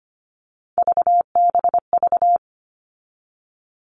Morse code, heard as a keyed tone, about 25 wpm, 700 Hz. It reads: VBV
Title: All Cut Numbers - 3-Digit 25wpm